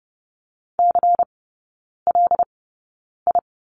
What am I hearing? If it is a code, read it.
Morse code, heard as a keyed tone, about 30 wpm, spaced as for 9 wpm, 700 Hz. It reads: CLI
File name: Lesson 18e - Letter m - Three Characters 30wpm